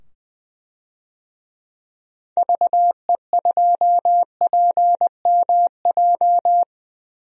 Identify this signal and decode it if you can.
Morse code, heard as a keyed tone, about 20 wpm, 700 Hz. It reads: VE2PMJ
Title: DXCC-100 Call Signs - 6 Characters - 20wpm